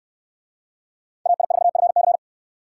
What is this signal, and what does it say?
Morse code, heard as a keyed tone, about 69 wpm, 700 Hz. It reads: AI4FC